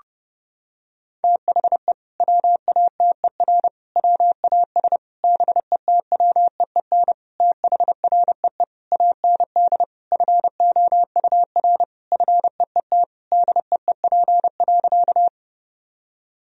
Morse code, heard as a keyed tone, about 32 wpm, 700 Hz. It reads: THE WATER WAS BETWEEN THREE AND FOUR FEET DEEP.
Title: Sentences from Top 1000 Words - Review Part 3 - 30wpm